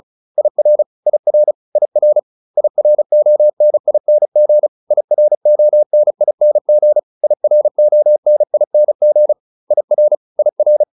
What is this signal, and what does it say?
Morse code, heard as a keyed tone, about 34 wpm, 600 Hz. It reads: IR IR IR IRONING IRONING IRONING IR IR